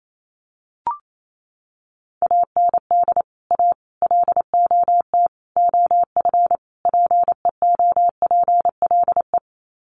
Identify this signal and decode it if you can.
Morse code, heard as a keyed tone, about 28 wpm, 700 Hz. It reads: AND A LOT OF PEOPLE
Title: Sets of 5 Words 28wpm